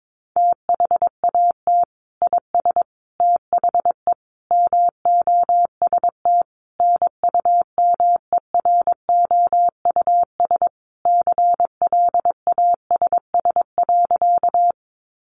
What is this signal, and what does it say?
Morse code, heard as a keyed tone, about 22 wpm, 700 Hz. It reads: THAT IS THE MOST NUMEROUS CLASS.
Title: Sentences from Top 1400 Words 22wpm